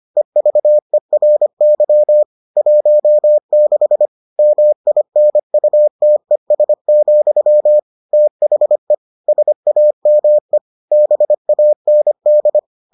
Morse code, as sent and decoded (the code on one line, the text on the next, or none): . ...- . .-. -.-- / .---- -.... / -- .. -. ..- - . ... --..-- / - .... / ... .- -- . / -... .- -. -..
EVERY 16 MINUTES, THE SAME BAND